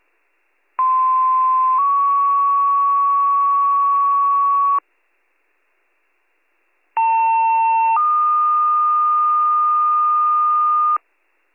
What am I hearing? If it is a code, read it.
Two-tone page, heard as a one-second tone, then a three-second tone, about 1040 then 1120 Hz, about 900 then 1220 Hz. It is heard from 0.8 to 11.0 s.